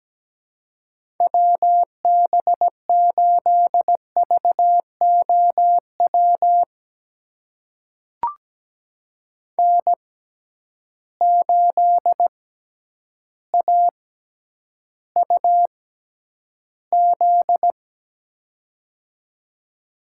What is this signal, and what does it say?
Morse code, heard as a keyed tone, about 17 wpm, spaced as for 9 wpm, 700 Hz. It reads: WB8VOW N8AUZ
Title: USA Call Signs - 6x Character Spacing - 17wpm